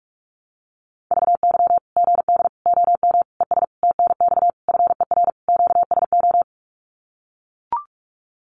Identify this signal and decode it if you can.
Morse code, heard as a keyed tone, about 44 wpm, 700 Hz. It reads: VY GD OM ES TNX FER QSO
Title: Common QSO Phrases 45wpm